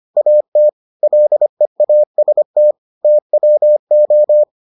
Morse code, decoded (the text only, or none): AT LEAST TWO